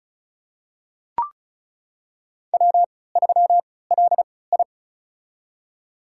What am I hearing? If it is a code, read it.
Morse code, heard as a keyed tone, about 36 wpm, spaced as for 20 wpm, 700 Hz. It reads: W3LI
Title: USA Call Signs - 4 Characters - 3x Character Spacing - 35wpm